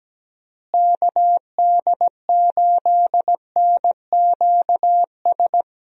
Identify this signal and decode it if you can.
Morse code, heard as a keyed tone, about 17 wpm, 700 Hz. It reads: KD8NQS
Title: USA Call Signs - 6 Characters - Repeated 3x - 17wpm